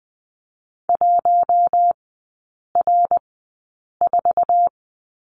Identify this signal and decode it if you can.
Morse code, heard as a keyed tone, about 20 wpm, spaced as for 8 wpm, 700 Hz. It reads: 1R4
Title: 053 Lesson 10e - Letter r - Three Characters 20wpm